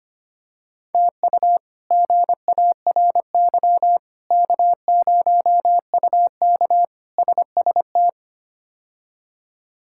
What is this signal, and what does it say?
Morse code, heard as a keyed tone, about 25 wpm, 700 Hz. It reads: TU GARY K0UK SST
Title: K1USN SST QSO Practice - 25wpm